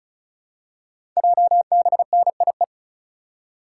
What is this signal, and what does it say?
Morse code, heard as a keyed tone, about 36 wpm, 700 Hz. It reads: JBNIE
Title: Code Groups - 5 Letters - 35wpm